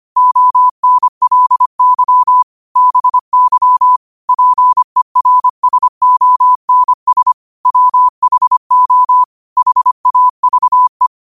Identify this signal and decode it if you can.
Morse code, heard as a keyed tone, about 25 wpm, 1000 Hz. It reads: ONLY BY PERSONS WHO HAVE